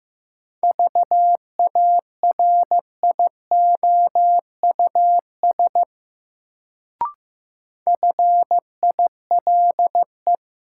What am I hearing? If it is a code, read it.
Morse code, heard as a keyed tone, about 15 wpm, 700 Hz. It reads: VARIOUS FILE
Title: Sentences from Top 1000 Words - Review Part 4 - New Word Drill - 15wpm